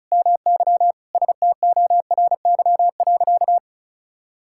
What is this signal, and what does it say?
Morse code, heard as a keyed tone, about 34 wpm, 700 Hz. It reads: MY STORY.
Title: Sentences from Top 1000 Words - Review Part 6 - 35wpm